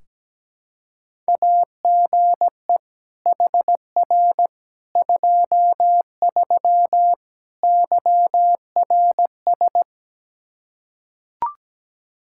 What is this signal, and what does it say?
Morse code, heard as a keyed tone, about 17 wpm, 700 Hz. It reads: AGE HR 23 YRS
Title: K1USN SST QSO Practice 17wpm